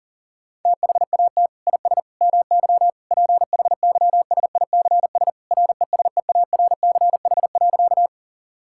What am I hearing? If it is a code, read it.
Morse code, heard as a keyed tone, about 37 wpm, 700 Hz. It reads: THAT IS MY PHYSICS RESEARCH.
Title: Sentences from Top 1000 Words - Review Part 5 - 40wpm